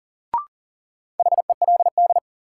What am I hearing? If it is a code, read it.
Morse code, heard as a keyed tone, about 40 wpm, 700 Hz. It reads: HELD
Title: Sentences from Top 1600 Words - New Word Drill Alt - 40wpm